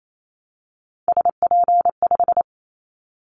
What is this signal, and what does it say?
Morse code, heard as a keyed tone, about 28 wpm, 700 Hz. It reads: SP5
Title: Callsign Prefixes for Top 25 DXCC - 28wpm